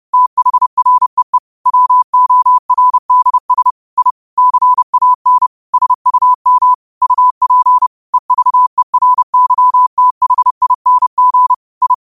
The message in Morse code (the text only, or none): THREE WORDS I CAN SUM UP EVERYTHING I